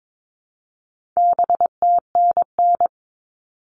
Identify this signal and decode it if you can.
Morse code, heard as a keyed tone, about 22 wpm, 700 Hz. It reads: BTNN
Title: All Cut Numbers - 4-Digit 22wpm